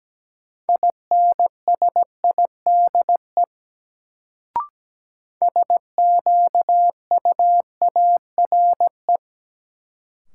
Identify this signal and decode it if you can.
Morse code, heard as a keyed tone, about 17 wpm, 700 Hz. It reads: INSIDE SQUARE
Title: Sentences from Top 1200 Words - New Word Drill Alt - 17wpm